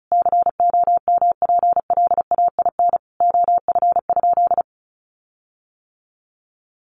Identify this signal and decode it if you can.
Morse code, heard as a keyed tone, about 34 wpm, 700 Hz. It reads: COMPLAIN OF?
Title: Sentences from Top 1300 Words 35wpm